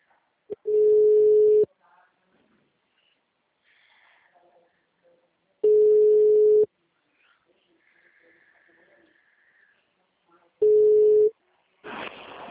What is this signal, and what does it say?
Ringback tone (Europe, 425 Hz).
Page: Sorocaba - SP